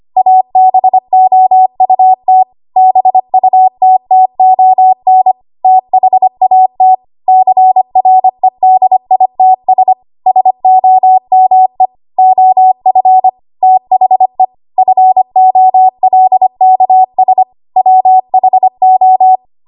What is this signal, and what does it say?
Morse code, heard as a keyed tone, about 25 wpm, 750 Hz. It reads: ABOUT BUTTON THAT CREDITS SOME OF THE FOLKS WHO
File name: Archive / 25 WPM / 25WPM